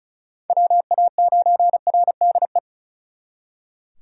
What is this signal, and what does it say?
Morse code, heard as a keyed tone, about 41 wpm, 700 Hz. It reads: WA9RDE
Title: Speed Racing - USA 2x3 Call Signs 35wpm